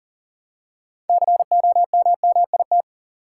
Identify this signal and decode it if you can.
Morse code, heard as a keyed tone, about 40 wpm, 700 Hz. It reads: COMMIT
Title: Six Letter Words - 40wpm